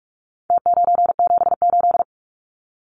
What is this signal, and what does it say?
Morse code, heard as a keyed tone, about 44 wpm, 700 Hz. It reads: T978
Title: Numbers - Contest Style - 4-Digit 45wpm